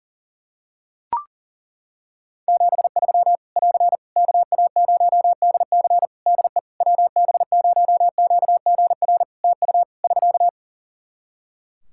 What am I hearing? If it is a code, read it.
Morse code, heard as a keyed tone, about 38 wpm, 700 Hz. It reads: 73 <AR> KA0DC DE WB0QGR TU <SK>
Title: Practice QSO - Exchange 4 - Ending - 40wpm